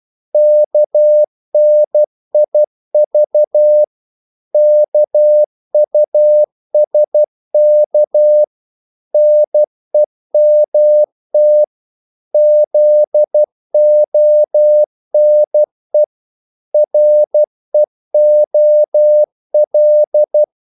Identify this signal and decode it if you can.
Morse code, heard as a keyed tone, about 12 wpm, 600 Hz. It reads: KNIV KUSK NEMT ZONE REOL
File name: Korte ord DK 12wpm